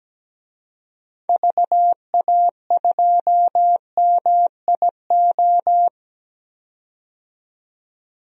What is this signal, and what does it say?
Morse code, heard as a keyed tone, about 17 wpm, 700 Hz. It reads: VA2MIO
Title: DXCC-100 Call Signs - 6 Characters - 17wpm